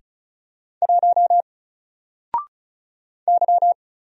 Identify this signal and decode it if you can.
Morse code, heard as a keyed tone, about 36 wpm, 700 Hz. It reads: 1 Y
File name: Single-Letter-Number-35wpm